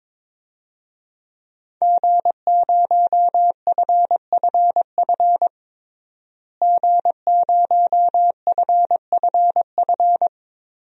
Morse code, heard as a keyed tone, about 22 wpm, 700 Hz. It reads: G0FFF G0FFF
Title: DXCC-100 Call Signs - 5 Characters - Repeated 3x - 22wpm